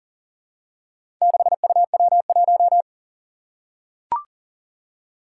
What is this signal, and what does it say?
Morse code, heard as a keyed tone, about 40 wpm, 700 Hz. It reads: 6UW1
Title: Four Challenging Letter-Number Combinations 40wpm